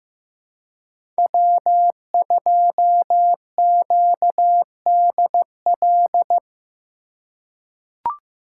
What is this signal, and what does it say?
Morse code, heard as a keyed tone, about 15 wpm, 700 Hz. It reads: W2QDL
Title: USA Call Signs - 5 Characters - 2x Character Spacing - 15wpm